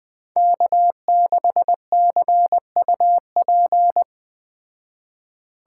Morse code, heard as a keyed tone, about 20 wpm, 700 Hz. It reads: K6CUP